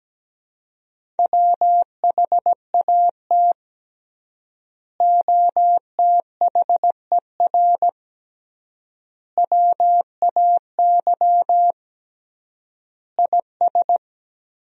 Morse code, heard as a keyed tone, about 17 wpm, 700 Hz. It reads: WHAT OTHER WAY IS
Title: 3x Word-Spacing - Sentences from Top 100 Words - 17wpm